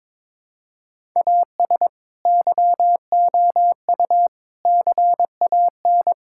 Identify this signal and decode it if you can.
Morse code, heard as a keyed tone, about 22 wpm, 700 Hz. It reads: AS YOU CAN
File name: practice / 22wpm / Sets of 7 Words - 22wpm